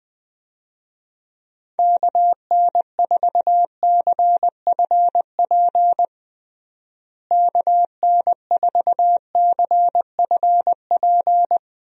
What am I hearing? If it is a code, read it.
Morse code, heard as a keyed tone, about 20 wpm, 700 Hz. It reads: KN4CFP KN4CFP
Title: USA Call Signs - 6 Characters - Repeated 3x - 20wpm